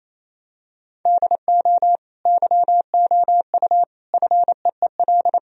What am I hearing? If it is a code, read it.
Morse code, heard as a keyed tone, about 28 wpm, 700 Hz. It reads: DO YOU FEEL
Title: Sentences from Top 1000 Words - Review Part 1 - 28wpm